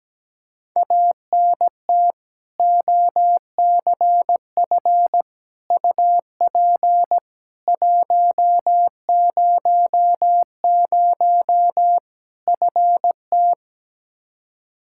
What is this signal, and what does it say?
Morse code, heard as a keyed tone, about 17 wpm, 700 Hz. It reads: ANT OCF UP 100 FT
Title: Practice QSO - Exchange 2 - Info - 17wpm